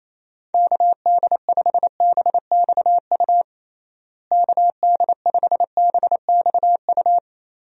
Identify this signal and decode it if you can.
Morse code, heard as a keyed tone, about 28 wpm, 700 Hz. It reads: KD5BXU KD5BXU
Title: USA Call Signs - 6 Characters - Repeated 3x - 28wpm